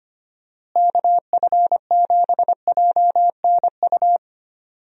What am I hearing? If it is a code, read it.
Morse code, heard as a keyed tone, about 25 wpm, 700 Hz. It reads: KF7JNU
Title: USA Call Signs - 6 Characters - Repeated 2x - 25wpm